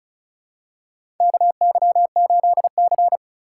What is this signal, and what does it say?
Morse code, heard as a keyed tone, about 36 wpm, 700 Hz. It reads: KY8C
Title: USA Call Signs - 4 Characters - 4x Character Spacing - 35wpm